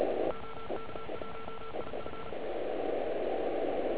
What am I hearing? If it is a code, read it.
Morse code, heard as a keyed tone, about 36 wpm, 1340 Hz, with noise. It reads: OK1IF